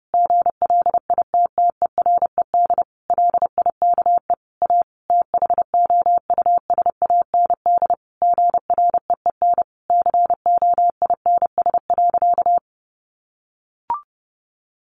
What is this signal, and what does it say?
Morse code, heard as a keyed tone, about 28 wpm, 700 Hz. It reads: GLITTERED LIKE A THOUSAND GREEN COINS.
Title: A Sound Of Thunder 30wpm